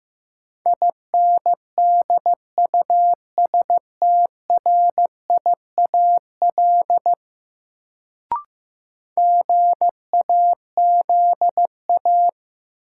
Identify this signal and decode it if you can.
Morse code, heard as a keyed tone, about 15 wpm, 700 Hz. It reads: INDUSTRIAL GAZA
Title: Sets of 5 Words - New Word Drill - 15wpm